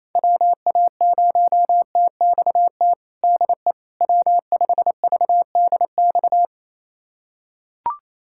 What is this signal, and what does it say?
Morse code, heard as a keyed tone, about 28 wpm, 700 Hz. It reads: WA0TXT DE W5VDX
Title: USA Callsign DE Callsign 28wpm